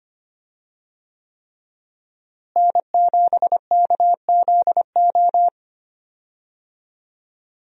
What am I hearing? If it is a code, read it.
Morse code, heard as a keyed tone, about 25 wpm, 700 Hz. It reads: N7KZO
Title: USA Call Signs - 5 Characters - Repeated 3x - 25wpm